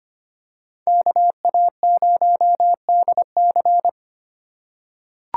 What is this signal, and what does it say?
Morse code, heard as a keyed tone, about 25 wpm, 700 Hz. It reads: KA0DC
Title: USA Call Signs - 5 Characters - 3x Character Spacing - 25wpm